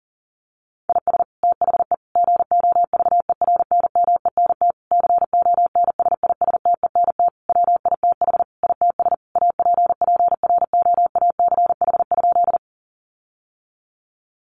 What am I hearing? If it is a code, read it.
Morse code, heard as a keyed tone, about 40 wpm, 700 Hz. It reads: IS THE GOVERNMENT CONSISTENT WITH ITS APPROACH?
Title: Sentences from Top 1500 Words 40wpm